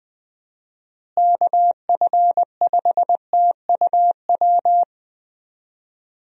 Morse code, heard as a keyed tone, about 21 wpm, 700 Hz. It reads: KF5TUW
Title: USA Call Signs - 6 Characters - Repeated 3x - 20wpm